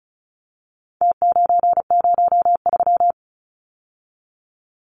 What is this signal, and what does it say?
Morse code, heard as a keyed tone, about 34 wpm, 700 Hz. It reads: T903
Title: Numbers - Contest Style - 4-Digit 35wpm